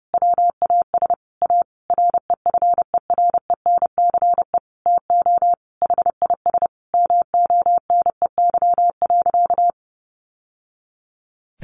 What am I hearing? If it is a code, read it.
Morse code, heard as a keyed tone, about 30 wpm, 700 Hz. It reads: WAS A REFERENCE TO HIS MONEY.
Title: Sentences from Top 1200 Words 30wpm